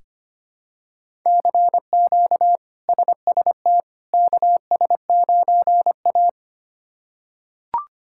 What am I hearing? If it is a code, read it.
Morse code, heard as a keyed tone, about 25 wpm, 700 Hz. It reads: CQ SST KS9A
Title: K1USN SST QSO Practice - 25wpm